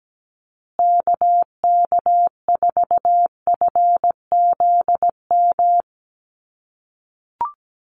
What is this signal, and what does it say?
Morse code, heard as a keyed tone, about 17 wpm, 700 Hz. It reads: KK4FZM
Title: USA Call Signs Repeated 2x 17wpm